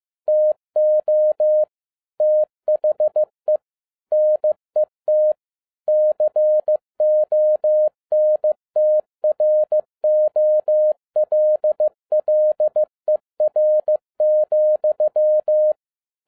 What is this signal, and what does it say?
Morse code, heard as a keyed tone, about 15 wpm, 610 Hz. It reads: TO THE NET CONTROLLER,
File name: 2012-01-mp3 / n17-SUMMARY OF FEA NET IN 2011-15wpm